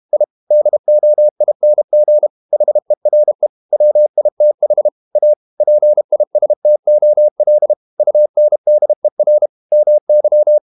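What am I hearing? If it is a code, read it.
Morse code, heard as a keyed tone, about 32 wpm, 600 Hz. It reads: I DOING HERE WITH A PISTOL UNDER MY
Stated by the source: Morse code practice at 32 wpm from CW med Gnister, reading Ken Follett's novel Never in English.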